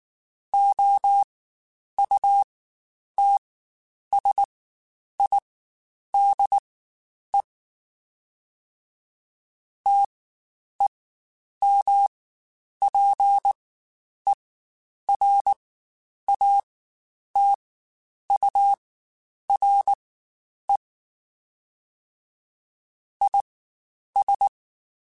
Morse code, heard as a keyed tone, about 19 wpm, spaced as for 9 wpm, 800 Hz. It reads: OUTSIDE TEMPERATURE IS
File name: Spaced Characters / Spaced Characters-mp3s / Weather 4 24 sp @18